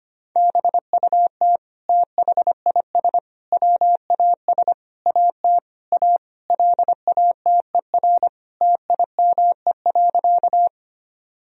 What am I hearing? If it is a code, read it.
Morse code, heard as a keyed tone, about 25 wpm, 700 Hz. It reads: BUT THIS WAS AT A LATER TIME.
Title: Sentences from Top 1000 Words - Review Part 4 - 25wpm